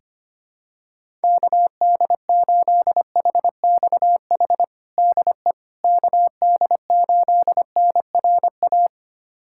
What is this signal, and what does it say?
Morse code, heard as a keyed tone, about 25 wpm, 700 Hz. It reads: KD8HXH DE KD8NRA
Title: USA Callsign DE Callsign 25wpm